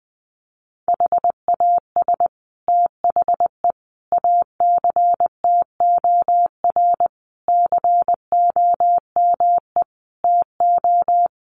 Morse code, heard as a keyed tone, about 20 wpm, 700 Hz. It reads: HAS THE ACTOR COME TO
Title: Sentences from Top 1500 Words 20wpm